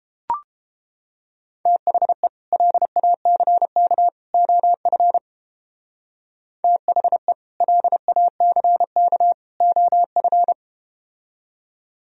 Morse code, heard as a keyed tone, about 31 wpm, 700 Hz. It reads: THE LACK OF THE LACK OF